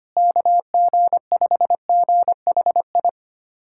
Morse code, heard as a keyed tone, about 25 wpm, 700 Hz. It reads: KG5GHI
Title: USA Callsign DE Callsign 25wpm